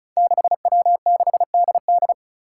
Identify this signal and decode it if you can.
Morse code, heard as a keyed tone, about 36 wpm, 700 Hz. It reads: 6W6DD
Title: Five Challenging Letter-Number Combinations 35wpm